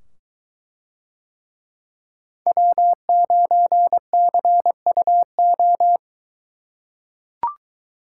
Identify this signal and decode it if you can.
Morse code, heard as a keyed tone, about 21 wpm, 700 Hz. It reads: W9CUO